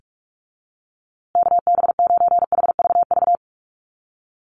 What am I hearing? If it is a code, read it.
Morse code, heard as a keyed tone, about 44 wpm, 700 Hz. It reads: KB9HVV
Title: USA Call Signs Repeated 3x 45wpm